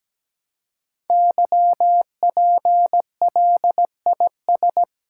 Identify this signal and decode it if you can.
Morse code, heard as a keyed tone, about 17 wpm, 700 Hz. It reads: YPLIS
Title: Code Groups - 5 Letters - 17wpm